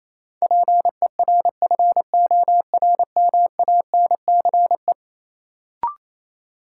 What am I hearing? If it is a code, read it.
Morse code, heard as a keyed tone, about 28 wpm, 700 Hz. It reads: PERFORMANCE
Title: Sentences from Top 1000 Words - New Word Drill - 28wpm